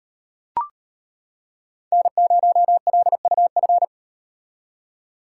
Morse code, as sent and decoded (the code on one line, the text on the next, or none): -. ----- .-.. ..- ..-.
N0LUF